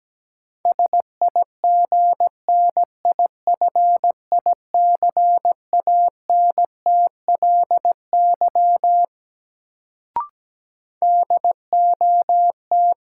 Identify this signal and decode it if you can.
Morse code, heard as a keyed tone, about 17 wpm, 700 Hz. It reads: SIGNIFICANTLY DOT
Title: Sentences from Top 1500 Words - New Word Drill - 17wpm